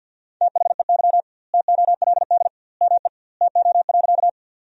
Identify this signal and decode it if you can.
Morse code, heard as a keyed tone, about 50 wpm, 700 Hz. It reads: THEY TOLD ME TO.
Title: 2x Word-Spacing - Sentences from Top 400 Words - 50wpm